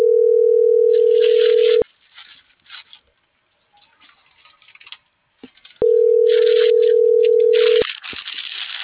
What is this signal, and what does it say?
Ringback tone (North America), 440 + 480 Hz.